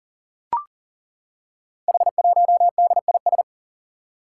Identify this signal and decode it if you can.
Morse code, heard as a keyed tone, about 38 wpm, 700 Hz. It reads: H1DIS
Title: Five Challenging Letter-Number Combinations 40wpm